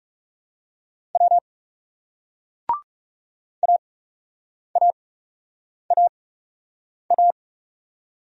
Morse code, heard as a keyed tone, about 40 wpm, spaced as for 8 wpm, 700 Hz. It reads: W AAAA